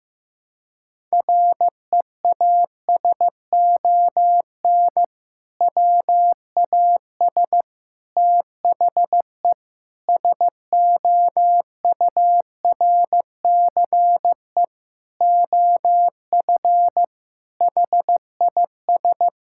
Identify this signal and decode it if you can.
Morse code, heard as a keyed tone, about 15 wpm, 700 Hz. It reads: REASON WAS THE SOURCE OF HIS
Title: Sentences from Top 1000 Words - Review Part 5 - 15wpm